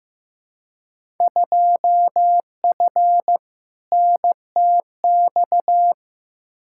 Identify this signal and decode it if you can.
Morse code, heard as a keyed tone, about 15 wpm, 700 Hz. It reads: 2F NTX
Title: Field Day Exchanges - 15wpm